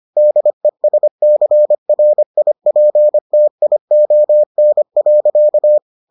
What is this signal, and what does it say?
Morse code, heard as a keyed tone, about 25 wpm, 600 Hz. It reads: DESCRIPTION.